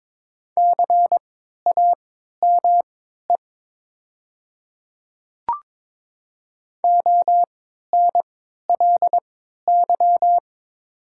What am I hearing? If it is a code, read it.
Morse code, heard as a keyed tone, about 21 wpm, spaced as for 12 wpm, 700 Hz. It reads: CAME ONLY
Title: 3x Character Spacing - Top 200 Words - 22wpm